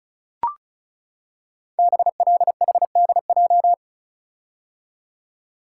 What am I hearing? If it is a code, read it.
Morse code, heard as a keyed tone, about 36 wpm, 700 Hz. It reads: BLHDJ
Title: Code Groups - 5 Letters - 35wpm